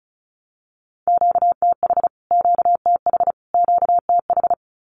Morse code, heard as a keyed tone, about 34 wpm, 700 Hz. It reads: QTH QTH QTH
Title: Q-Codes Repeated 3x 35wpm